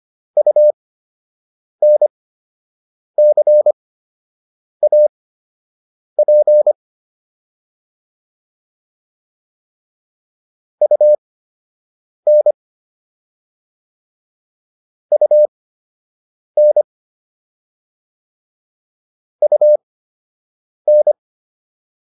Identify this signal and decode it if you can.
Morse code, heard as a keyed tone, about 25 wpm, spaced as for 7 wpm, 600 Hz. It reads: UNCAP UN UN UN